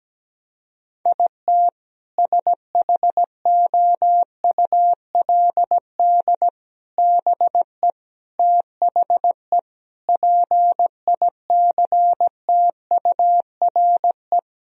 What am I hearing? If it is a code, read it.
Morse code, heard as a keyed tone, about 17 wpm, 700 Hz. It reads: IT SHOULD BE THE PICTURE
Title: Sentences from Top 1000 Words - Review Part 1 - 17wpm